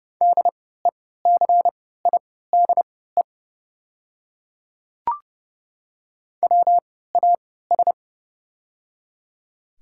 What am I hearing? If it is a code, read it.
Morse code, heard as a keyed tone, about 30 wpm, spaced as for 17 wpm, 700 Hz. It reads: DECIDE WAS
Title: 3x Character Spacing - Top 500 Words - 30wpm